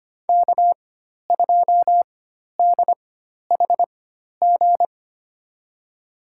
Morse code, heard as a keyed tone, about 25 wpm, spaced as for 12 wpm, 700 Hz. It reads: K2DHG
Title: USA Call Signs - 5 Characters - 4x Character Spacing - 25wpm